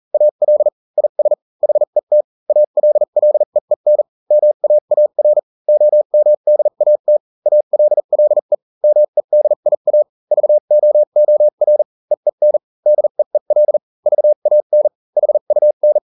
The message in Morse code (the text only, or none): AL IS HET ALLEEN MAAR OMDAT ALLE MEDIA VOOR EEN DEEL VAN HUN